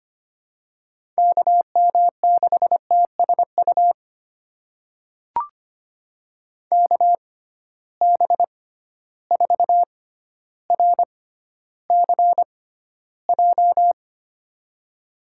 Morse code, heard as a keyed tone, about 25 wpm, spaced as for 13 wpm, 700 Hz. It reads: KM6TSU KB4RCJ
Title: USA Call Signs - 6 Characters - 6x Character Spacing - 25wpm